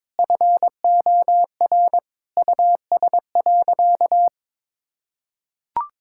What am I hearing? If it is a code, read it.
Morse code, heard as a keyed tone, about 22 wpm, 700 Hz. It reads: FOR US.
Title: Sentences from Top 1000 Words - Review Part 6 - 22wpm